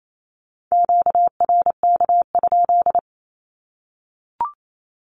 Morse code, heard as a keyed tone, about 28 wpm, 700 Hz. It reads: QRK?
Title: Q-Codes 28wpm